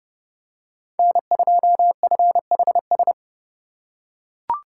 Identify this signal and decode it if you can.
Morse code, heard as a keyed tone, about 30 wpm, 700 Hz. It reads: N2FHS
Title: USA Call Signs - 5 Characters - 2x Character Spacing - 30wpm